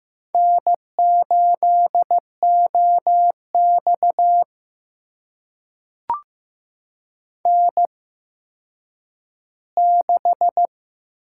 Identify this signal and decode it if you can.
Morse code, heard as a keyed tone, about 15 wpm, spaced as for 9 wpm, 700 Hz. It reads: N8OX N6
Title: USA Call Signs - 4 Characters - 8x Character Spacing - 15wpm